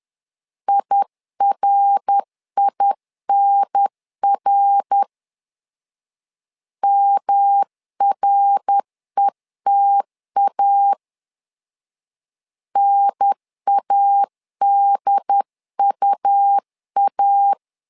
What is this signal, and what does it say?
Morse code, heard as a keyed tone, about 10 wpm, 800 Hz. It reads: IRINR MRETA NADUA
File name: LEZIONI / 3 MWRO / 3 MWR 10wpm 1952